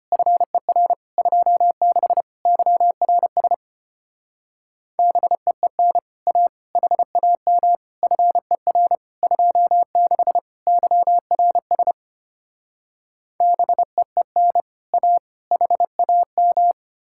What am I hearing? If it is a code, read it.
Morse code, heard as a keyed tone, about 33 wpm, 700 Hz. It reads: FER 26 YRS BEEN A HAM FER 26 YRS BEEN A HAM